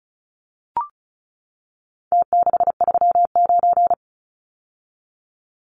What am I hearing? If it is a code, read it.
Morse code, heard as a keyed tone, about 34 wpm, 700 Hz. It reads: T639
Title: Numbers - Contest Style - 4-Digit 35wpm